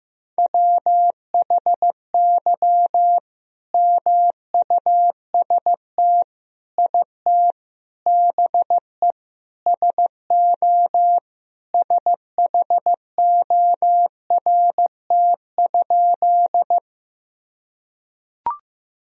Morse code, heard as a keyed tone, about 16 wpm, 700 Hz. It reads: WHY MUST IT BE SO SHORT?
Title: Sentences from Top 1000 Words - Review Part 2 - 15wpm